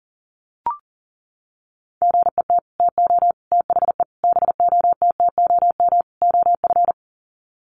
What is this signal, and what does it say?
Morse code, heard as a keyed tone, about 40 wpm, 700 Hz. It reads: GET TO THE BOTTOM OF
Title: Sets of 5 Words Encore 40wpm